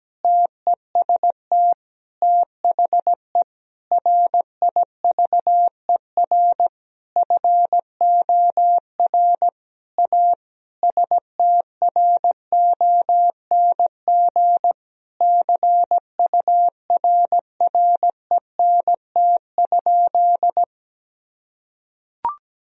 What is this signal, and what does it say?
Morse code, heard as a keyed tone, about 17 wpm, 700 Hz. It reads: TEST THE RIVER FOR A STRONG CURRENT?
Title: Sentences from Top 1000 Words - Review Part 4 - 17wpm